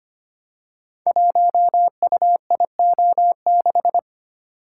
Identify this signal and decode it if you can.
Morse code, heard as a keyed tone, about 25 wpm, 700 Hz. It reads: 1UIO6
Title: Code Groups - 5 Letter:Numbers - 25wpm